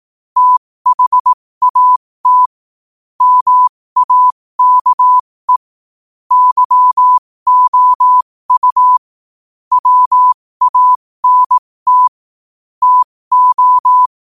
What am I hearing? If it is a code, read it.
Morse code, heard as a keyed tone, about 18 wpm, spaced as for 15 wpm, 1000 Hz. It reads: THAT MAKE YOU WANT TO
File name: qotdfiles / 15wpm